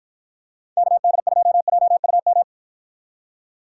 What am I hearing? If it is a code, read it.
Morse code, heard as a keyed tone, about 49 wpm, 700 Hz. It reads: KD2JUM